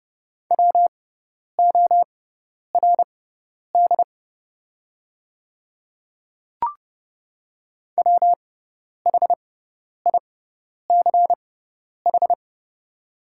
Morse code, heard as a keyed tone, about 30 wpm, spaced as for 10 wpm, 700 Hz. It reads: WORD WHICH
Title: 6x Character Spacing - Top 100 Words - 30wpm